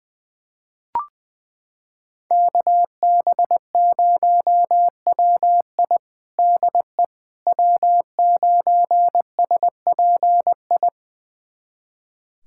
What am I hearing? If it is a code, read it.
Morse code, heard as a keyed tone, about 20 wpm, 700 Hz. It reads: KB0WI DE W9SPI
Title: USA Callsign DE Callsign - 5 Characters - 20wpm